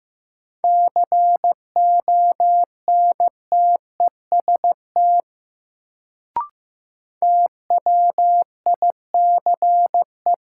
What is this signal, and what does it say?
Morse code, heard as a keyed tone, about 15 wpm, 700 Hz. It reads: CONTEST TWICE
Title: Sentences from Top 1600 Words - New Word Drill - 15wpm